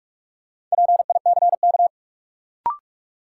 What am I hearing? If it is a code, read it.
Morse code, heard as a keyed tone, about 45 wpm, 700 Hz. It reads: PICK
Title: Sentences from Top 1000 Words - New Word Drill - 45wpm